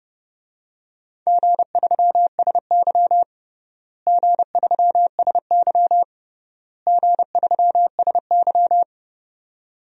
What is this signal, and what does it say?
Morse code, heard as a keyed tone, about 30 wpm, 700 Hz. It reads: G3SY G3SY G3SY
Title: DXCC-100 Call Signs - 4 Characters - Repeated 3x - 30wpm